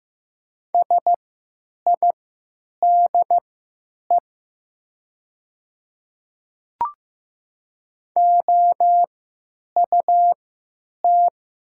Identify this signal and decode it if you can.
Morse code, heard as a keyed tone, about 15 wpm, spaced as for 9 wpm, 700 Hz. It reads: SIDE OUT
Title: 3x Character Spacing - Top 100 Words - 15wpm